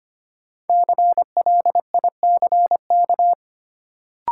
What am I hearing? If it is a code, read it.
Morse code, heard as a keyed tone, about 25 wpm, 700 Hz. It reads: CLICK
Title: Sentences from Top 1100 Words - New Word Drill - 25wpm